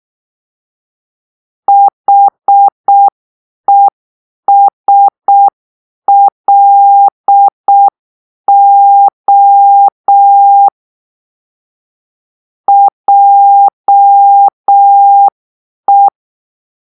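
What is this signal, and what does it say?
Morse code, heard as a keyed tone, about 6 wpm, 800 Hz. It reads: HESLO JE
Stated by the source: Morse-code cipher made by a player team for the Kachna puzzle game.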